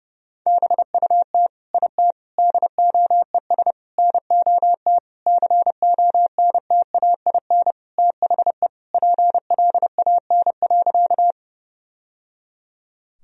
Morse code, heard as a keyed tone, about 30 wpm, 700 Hz. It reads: BUT IT DOES NOT CONTAIN THE PLAN.
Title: Sentences from Top 1000 Words - Review Part 2 - 30wpm